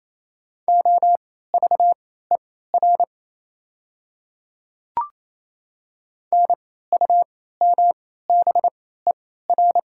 Morse code, heard as a keyed tone, about 27 wpm, spaced as for 16 wpm, 700 Hz. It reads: OVER NUMBER
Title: 3x Character Spacing - Top 200 Words - 28wpm